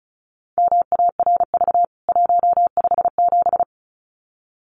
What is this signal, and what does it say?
Morse code, heard as a keyed tone, about 34 wpm, 700 Hz. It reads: MARV 157
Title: CWT Exchange 35wpm